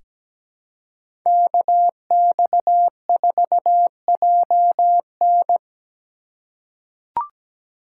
Morse code, heard as a keyed tone, about 17 wpm, 700 Hz. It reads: KX4JN
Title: USA Call Signs - 5 Characters - 3x Character Spacing - 17wpm